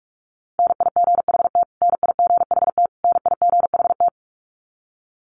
Morse code, heard as a keyed tone, about 44 wpm, 700 Hz. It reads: NIGHT NIGHT NIGHT
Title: Top 500 Words Repeated 45wpm